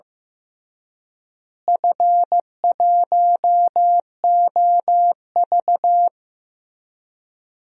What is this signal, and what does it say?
Morse code, heard as a keyed tone, about 15 wpm, 700 Hz. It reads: F1OV
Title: DXCC-100 Call Signs - 4 Characters - Repeated 3x - 15wpm